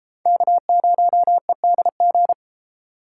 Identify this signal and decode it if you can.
Morse code, heard as a keyed tone, about 31 wpm, 700 Hz. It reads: K0EDG